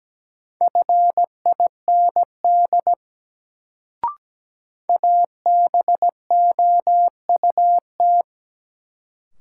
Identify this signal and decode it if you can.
Morse code, heard as a keyed tone, about 17 wpm, 700 Hz. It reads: FIND ABOUT
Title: Sentences from Top 100 Words - New Word Drill - 17wpm